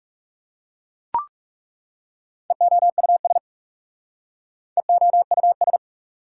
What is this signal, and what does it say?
Morse code, heard as a keyed tone, about 39 wpm, 690 Hz. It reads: EOUS EOUS